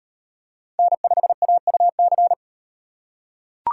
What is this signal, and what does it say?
Morse code, heard as a keyed tone, about 38 wpm, 700 Hz. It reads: N5AUC